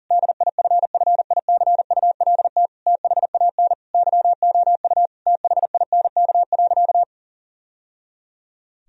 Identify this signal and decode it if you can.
Morse code, heard as a keyed tone, about 40 wpm, 700 Hz. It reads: DIFFICULT THAN YOU THINK.
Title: Sentences from Top 1000 Words - Review Part 3 - 40wpm